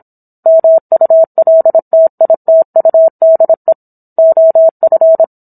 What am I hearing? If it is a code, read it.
Morse code, heard as a keyed tone, about 27 wpm, 650 Hz. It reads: MULTITUDE OF